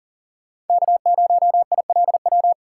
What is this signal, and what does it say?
Morse code, heard as a keyed tone, about 52 wpm, 700 Hz. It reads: K0ILW